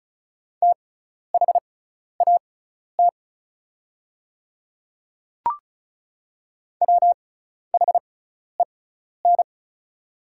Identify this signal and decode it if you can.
Morse code, heard as a keyed tone, about 34 wpm, spaced as for 12 wpm, 700 Hz. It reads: THAT WHEN
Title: 6x Character Spacing - Top 100 Words - 35wpm